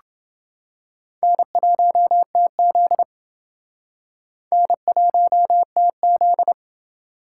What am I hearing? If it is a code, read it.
Morse code, heard as a keyed tone, about 28 wpm, 700 Hz. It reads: N1TZ N1TZ